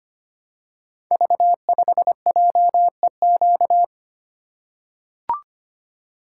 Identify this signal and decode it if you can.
Morse code, heard as a keyed tone, about 25 wpm, 700 Hz. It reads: V5JEQ
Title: Code Groups - 5 Letter:Numbers - 25wpm